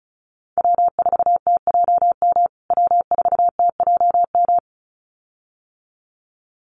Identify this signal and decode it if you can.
Morse code, heard as a keyed tone, about 34 wpm, 700 Hz. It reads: W4TJM W4TJM
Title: USA Call Signs Repeated 2x 35wpm